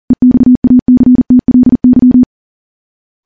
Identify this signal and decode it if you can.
Morse code, heard as a keyed tone, about 40 wpm, 250 Hz. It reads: EXACTLY
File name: morse / morse-training-40WPM